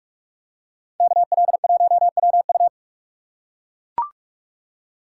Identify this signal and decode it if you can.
Morse code, heard as a keyed tone, about 42 wpm, 700 Hz. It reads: KL1WU